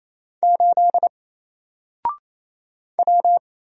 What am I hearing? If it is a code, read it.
Morse code, heard as a keyed tone, about 27 wpm, 700 Hz. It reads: 8 W
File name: Single-Letter-Number-28wpm